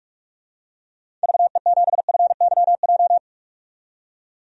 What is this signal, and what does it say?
Morse code, heard as a keyed tone, about 46 wpm, 700 Hz. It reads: VE7FYJ
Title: DXCC-100 Call Signs - 6 Characters - 45wpm